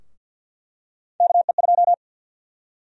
Morse code, heard as a keyed tone, about 52 wpm, 700 Hz. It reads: KE2
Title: USA Call Sign Prefixes - Repeated 2x - 50wpm